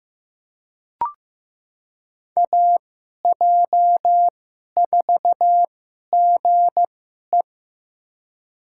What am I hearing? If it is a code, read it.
Morse code, heard as a keyed tone, about 15 wpm, spaced as for 11 wpm, 700 Hz. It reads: AJ4GE
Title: USA Call Signs - 5 Characters - 2x Character Spacing - 15wpm